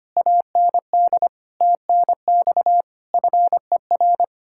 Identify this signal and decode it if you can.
Morse code, heard as a keyed tone, about 33 wpm, 700 Hz. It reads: AND TNX FER